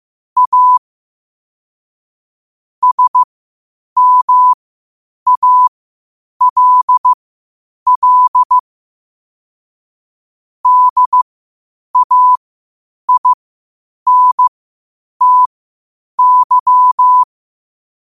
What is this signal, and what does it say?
Morse code, heard as a keyed tone, about 15 wpm, spaced as for 8 wpm, 1000 Hz. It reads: A SMALL DAINTY
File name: qotdfiles / 8wpm